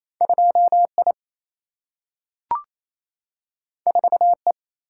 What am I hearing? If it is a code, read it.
Morse code, heard as a keyed tone, about 28 wpm, 700 Hz. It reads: 2I 4E
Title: Single-Letter Single-Number 28wpm